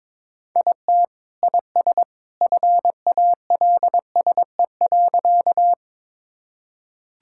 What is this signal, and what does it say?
Morse code, heard as a keyed tone, about 21 wpm, 700 Hz. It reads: IT IS FALSE.
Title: Sentences from Top 1600 Words 22wpm